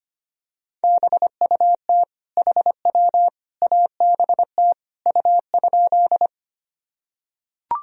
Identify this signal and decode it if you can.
Morse code, heard as a keyed tone, about 25 wpm, 700 Hz. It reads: BUT HW ABT U?
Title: Sentences from Top 100 Words and Abbreviations 25wpm